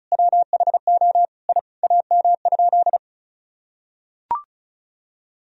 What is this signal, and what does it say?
Morse code, heard as a keyed tone, about 34 wpm, 700 Hz. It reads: WHO I AM?
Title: Sentences from Top 1000 Words - Review Part 2 - 35wpm